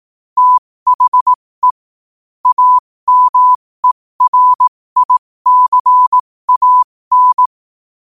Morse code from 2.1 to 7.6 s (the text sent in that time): AMERICAN